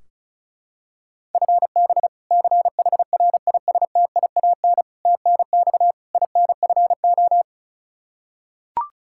Morse code, heard as a keyed tone, about 36 wpm, 700 Hz. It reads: FB CHRISTIAN TNX INFO
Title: K1USN SST QSO Practice 35wpm